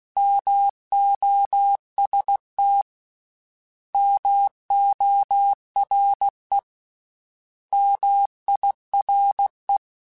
Morse code, heard as a keyed tone, about 16 wpm, 800 Hz. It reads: MOST MORE MIRE
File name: Misc / Misc-mp3s / Misc @15